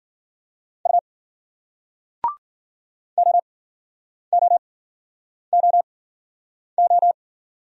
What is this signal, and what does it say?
Morse code, heard as a keyed tone, about 54 wpm, spaced as for 9 wpm, 700 Hz. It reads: U OOOO